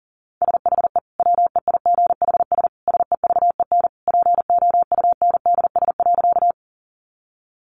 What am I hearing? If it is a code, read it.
Morse code, heard as a keyed tone, about 40 wpm, 700 Hz. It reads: SHE WEIGHS SEVEN POUNDS.
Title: Easy Phrases 40wpm